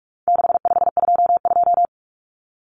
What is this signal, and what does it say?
Morse code, heard as a keyed tone, about 44 wpm, 700 Hz. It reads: 6522
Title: Common Cut Numbers - 4-Digit 45wpm